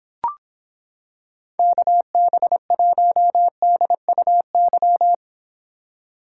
Morse code, heard as a keyed tone, about 26 wpm, 700 Hz. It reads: KB1DUY